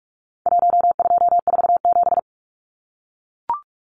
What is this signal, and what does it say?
Morse code, heard as a keyed tone, about 44 wpm, 700 Hz. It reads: 1247
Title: Numbers - Contest Style - 4-Digit 45wpm